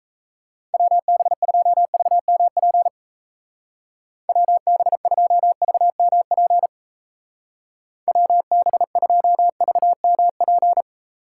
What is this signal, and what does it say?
Morse code, heard as a keyed tone, about 36 wpm, 700 Hz. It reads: WB2VMP WB2VMP WB2VMP